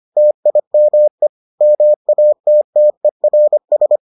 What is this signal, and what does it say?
Morse code, heard as a keyed tone, about 25 wpm, 600 Hz. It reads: TIME MATTERS